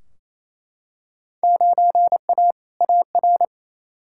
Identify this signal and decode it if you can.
Morse code, heard as a keyed tone, about 28 wpm, 700 Hz. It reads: 9A AR
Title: Field Day Exchanges Encore - 28wpm